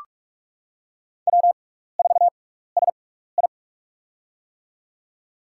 Morse code, heard as a keyed tone, about 44 wpm, spaced as for 15 wpm, 700 Hz. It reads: W4SI
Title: USA Call Signs - 4 Characters - 6x Character Spacing - 45wpm